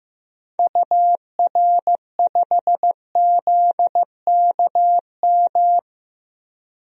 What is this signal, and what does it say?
Morse code, heard as a keyed tone, about 15 wpm, 700 Hz. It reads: UR5ZKM
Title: DXCC-100 Call Signs - 6 Characters - Repeated 3x - 15wpm